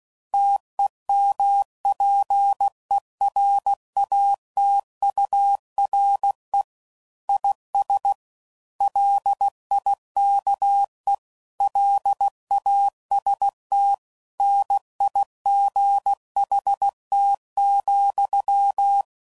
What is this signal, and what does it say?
Morse code, heard as a keyed tone, about 16 wpm, 800 Hz. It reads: TEMPERATURE IS LIKE LAST NIGHT,
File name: CW Code Groups @15WPM / Weather Sets / Weather 4 17